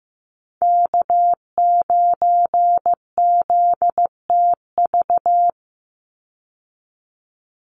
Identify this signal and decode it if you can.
Morse code, heard as a keyed tone, about 15 wpm, 700 Hz. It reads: K9ZTV
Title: Call Signs Repeated 2x - 15wpm